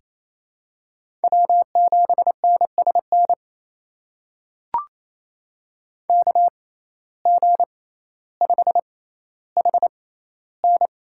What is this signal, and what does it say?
Morse code, heard as a keyed tone, about 28 wpm, spaced as for 14 wpm, 700 Hz. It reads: W7NSN KG5HN
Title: USA Call Signs - 6x Character Spacing - 28wpm